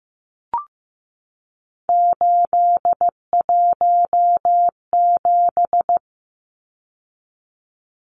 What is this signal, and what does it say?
Morse code, heard as a keyed tone, about 15 wpm, 700 Hz. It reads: 817
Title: Numbers - 3-Digit 15wpm